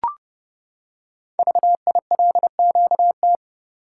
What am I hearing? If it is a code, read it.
Morse code, heard as a keyed tone, about 30 wpm, 690 Hz. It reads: VILQT
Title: Code Groups - 5 Letters - 30wpm